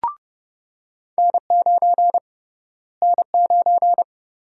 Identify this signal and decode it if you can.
Morse code, heard as a keyed tone, about 30 wpm, 700 Hz. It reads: N9 N9
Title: USA Call Sign Prefixes - Repeated 2x - 30wpm